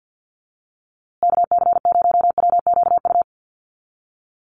Morse code, heard as a keyed tone, about 51 wpm, 700 Hz. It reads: KC0WQU
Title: USA Call Signs Repeated 2x 50wpm